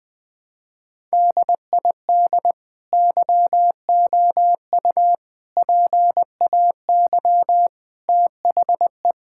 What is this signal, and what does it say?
Morse code, heard as a keyed tone, about 20 wpm, 700 Hz. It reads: DID YOU PAY THE
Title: Sentences from Top 1000 Words - Review Part 5 - 20wpm